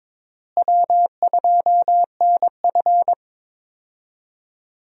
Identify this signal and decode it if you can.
Morse code, heard as a keyed tone, about 22 wpm, 700 Hz. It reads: W2NF
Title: USA Call Signs - 4 Characters - Repeated 3x - 22wpm